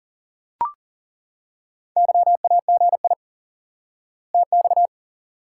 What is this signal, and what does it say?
Morse code, heard as a keyed tone, about 40 wpm, 700 Hz. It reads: YAGI TX
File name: IQE Course 3 - Lesson 89 - YAGI - Two Elements - 40wpm